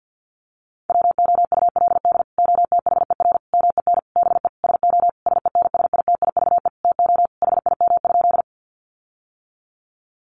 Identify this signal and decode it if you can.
Morse code, heard as a keyed tone, about 50 wpm, 700 Hz. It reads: WOULD OTHER MEN BE SO SENSITIVE TO HIM?
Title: Sentences from Top 1400 Words 50wpm